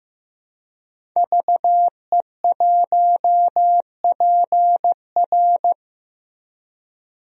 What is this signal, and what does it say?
Morse code, heard as a keyed tone, about 15 wpm, 700 Hz. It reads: VE1PR
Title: DXCC-100 Call Signs - 5 Characters - Repeated 3x - 15wpm